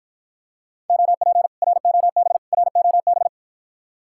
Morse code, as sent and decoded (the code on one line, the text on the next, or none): --- .--. / .-. --- -... / .-. --- -...
OP ROB ROB